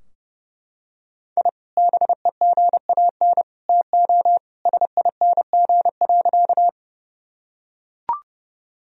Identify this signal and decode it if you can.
Morse code, heard as a keyed tone, about 32 wpm, 700 Hz. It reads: I BEGAN TO SING.
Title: Sentences from Top 1000 Words - Review Part 2 - 30wpm